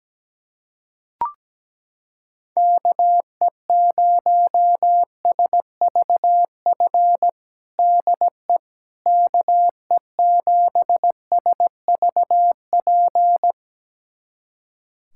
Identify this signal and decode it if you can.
Morse code, heard as a keyed tone, about 17 wpm, 700 Hz. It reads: KE0SVF DE KE7SVP
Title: USA Callsign DE Callsign - 6 Characters - 17wpm